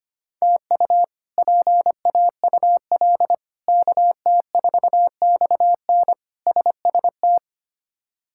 Morse code, heard as a keyed tone, about 25 wpm, 700 Hz. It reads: TU PAUL KT4XN SST
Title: K1USN SST QSO Practice - 25wpm